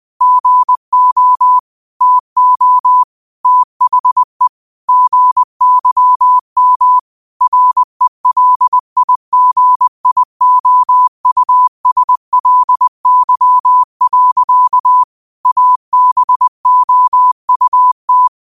Morse code, heard as a keyed tone, about 20 wpm, 1000 Hz. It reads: GO TO THE GYM RELIGIOUSLY. ABOUT
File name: qotdfiles / 20wpm